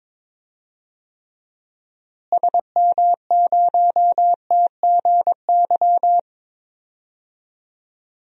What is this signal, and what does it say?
Morse code, heard as a keyed tone, about 21 wpm, 700 Hz. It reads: SM0TGY
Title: DXCC-100 Call Signs - 6 Characters - Repeated 2x - 22wpm